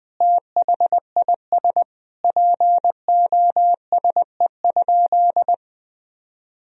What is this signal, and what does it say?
Morse code, heard as a keyed tone, about 20 wpm, 700 Hz. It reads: THIS POSE?
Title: Sentences from Top 1000 Words - Review Part 2 - 20wpm